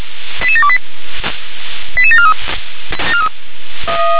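Five-tone selective call (ZVEI), heard as quick runs of five tones, tones of about 70 ms. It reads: 88728 88743
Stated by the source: ZVEI five-tone selective call received by SDR.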